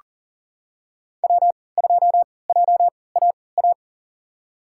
Morse code, heard as a keyed tone, about 40 wpm, spaced as for 23 wpm, 700 Hz. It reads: W2JAA
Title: USA Call Signs - 5 Characters - 3x Character Spacing - 40wpm